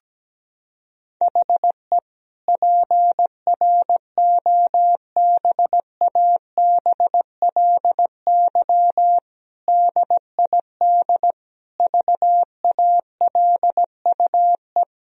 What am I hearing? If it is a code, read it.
Morse code, heard as a keyed tone, about 17 wpm, 700 Hz. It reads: HE PROBABLY DID VALUE
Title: Sentences from Top 1000 Words - Review Part 4 - 17wpm